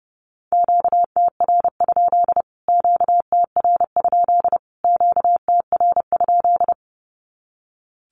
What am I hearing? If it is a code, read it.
Morse code, heard as a keyed tone, about 30 wpm, 700 Hz. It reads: QTR? QTR? QTR?
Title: Q-Codes Repeated 3x 30wpm